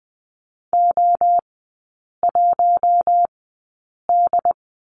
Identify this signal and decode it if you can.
Morse code, heard as a keyed tone, about 20 wpm, spaced as for 8 wpm, 700 Hz. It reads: O1D
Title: 127 Lesson 21e - Number 6 - Three Characters 20wpm